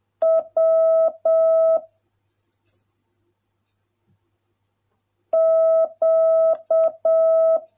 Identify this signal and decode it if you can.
Morse code, heard as a keyed tone, about 7 wpm, 650 Hz. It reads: W Q